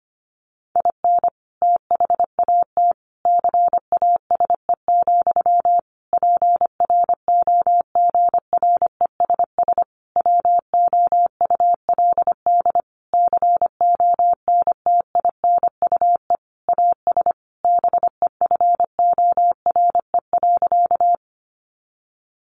Morse code, heard as a keyed tone, about 25 wpm, 700 Hz. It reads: IN THAT CASE, PROGRESS WOULD CONTINUE AS BEFORE.
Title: Sentences from Top 1400 Words 25wpm